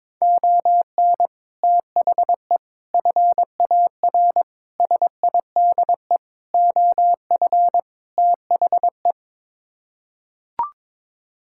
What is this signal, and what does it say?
Morse code, heard as a keyed tone, about 21 wpm, 700 Hz. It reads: ON THE FAR SIDE OF THE
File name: practice / 22wpm / Sets of 6 Words - 22wpm